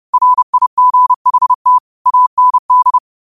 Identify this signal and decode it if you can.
Morse code, heard as a keyed tone, about 30 wpm, 1000 Hz. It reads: RIGHT AND